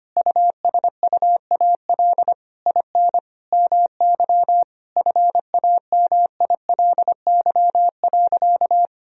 Morse code, heard as a keyed tone, about 25 wpm, 700 Hz. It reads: USUAL IN MY FAMILY.
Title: Sentences from Top 1000 Words - Review Part 2 - 25wpm